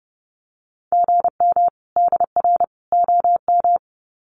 Morse code, heard as a keyed tone, about 30 wpm, 700 Hz. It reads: GM DR OM
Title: Common QSO Phrases 30wpm